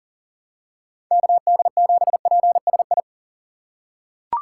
Morse code, heard as a keyed tone, about 40 wpm, 700 Hz. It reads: KD7PSI
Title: USA Call Signs - 6 Characters - 4x Character Spacing - 40wpm